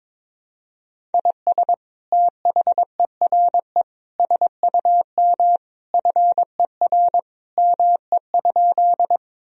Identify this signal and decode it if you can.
Morse code, heard as a keyed tone, about 22 wpm, 700 Hz. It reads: IS THERE SUM FER ME?
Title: Sentences from Top 200 Words and Abbreviations 22wpm